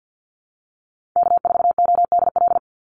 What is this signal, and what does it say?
Morse code, heard as a keyed tone, about 50 wpm, 700 Hz. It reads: K4ODL
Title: Call Signs - 50wpm